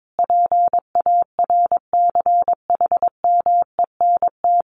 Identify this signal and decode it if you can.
Morse code, heard as a keyed tone, about 22 wpm, 700 Hz. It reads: PARCHMENT
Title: A Sound Of Thunder 22wpm